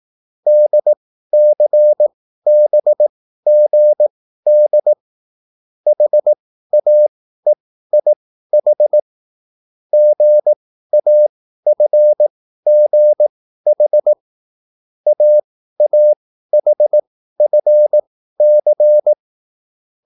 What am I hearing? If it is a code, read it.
Morse code, heard as a keyed tone, about 18 wpm, spaced as for 13 wpm, 600 Hz. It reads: DCBGD HAEIH GAFGH AAHFC
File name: QSL / CW lesson